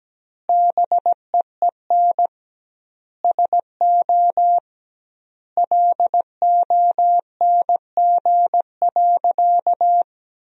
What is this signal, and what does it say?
Morse code, heard as a keyed tone, about 17 wpm, 700 Hz. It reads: BEEN SO LONG.
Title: 2x Word-Spacing - Sentences from Top 100 Words - 17wpm